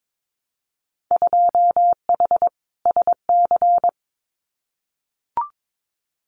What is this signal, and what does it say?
Morse code, heard as a keyed tone, about 22 wpm, 700 Hz. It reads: 2H SC
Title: Winter Field Day Exchanges 22wpm